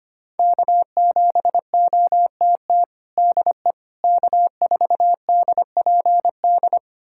Morse code, heard as a keyed tone, about 25 wpm, 700 Hz. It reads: K7OTT DE K4DPD
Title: USA Callsign DE Callsign - 5 Characters - 25wpm